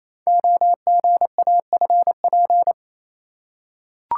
Morse code, heard as a keyed tone, about 28 wpm, 700 Hz. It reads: OGAFP
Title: Code Groups - 5 Letters - 28wpm